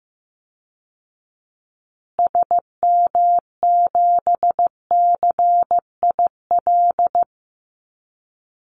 Morse code, heard as a keyed tone, about 15 wpm, 700 Hz. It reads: SM7CIL
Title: Call Signs Repeated 2x - 15wpm